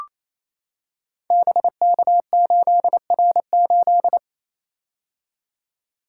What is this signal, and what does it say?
Morse code, heard as a keyed tone, about 27 wpm, 700 Hz. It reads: BK8R8
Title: Code Groups - 5 Letter:Numbers - 28wpm